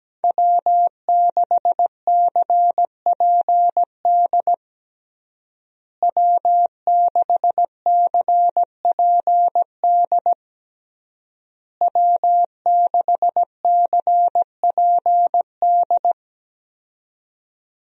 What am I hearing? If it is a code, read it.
Morse code, heard as a keyed tone, about 17 wpm, 700 Hz. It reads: W6CPD W6CPD W6CPD
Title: USA Call Signs - 5 Characters - Repeated 3x - 17wpm